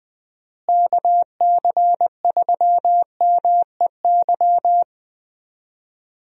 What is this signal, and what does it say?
Morse code, heard as a keyed tone, about 20 wpm, 700 Hz. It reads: KC3MEY
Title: USA Call Signs - 6 Characters - Repeated 3x - 20wpm